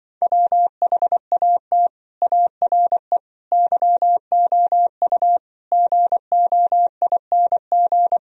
Morse code, heard as a keyed tone, about 25 wpm, 700 Hz. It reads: WHAT ARE YOU GOING